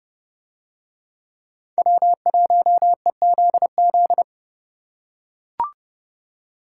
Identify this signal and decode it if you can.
Morse code, heard as a keyed tone, about 30 wpm, 700 Hz. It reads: W1EZZ